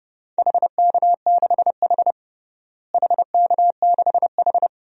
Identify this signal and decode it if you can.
Morse code, heard as a keyed tone, about 32 wpm, 700 Hz. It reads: HK6H HK6H
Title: DXCC-100 Call Signs - 4 Characters - Repeated 2x - 30wpm